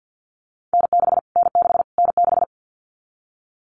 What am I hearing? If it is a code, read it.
Morse code, heard as a keyed tone, about 50 wpm, 700 Hz. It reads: N6 N6 N6
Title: Callsign Prefixes for Top 25 DXCC - Repeated 3x - 50wpm